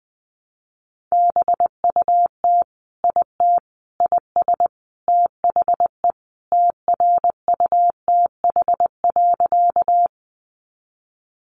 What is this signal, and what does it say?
Morse code, heard as a keyed tone, about 20 wpm, 700 Hz. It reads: BUT IT IS THE TRUTH.
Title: Sentences from Top 800 Words 20wpm